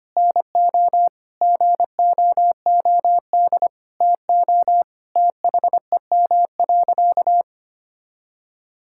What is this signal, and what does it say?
Morse code, heard as a keyed tone, about 25 wpm, 700 Hz. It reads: NO GOOD TO THEM.
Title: Sentences from Top 1000 Words - Review Part 6 - 25wpm